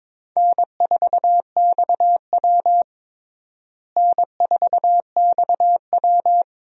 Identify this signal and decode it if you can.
Morse code, heard as a keyed tone, about 22 wpm, 700 Hz. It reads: N4XW N4XW
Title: USA Call Signs - 4 Characters - Repeated 2x - 22wpm